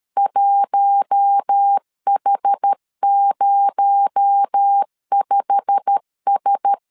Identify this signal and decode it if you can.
Morse code, heard as a keyed tone, about 13 wpm, 800 Hz. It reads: 1H05S
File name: LEZIONI / 8 YJ45 / 8 YJ45 12wpm 2151